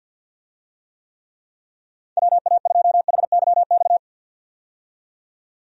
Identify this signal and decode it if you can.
Morse code, heard as a keyed tone, about 47 wpm, 700 Hz. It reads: WA2HYX
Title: USA Call Signs - 6 Characters - 50wpm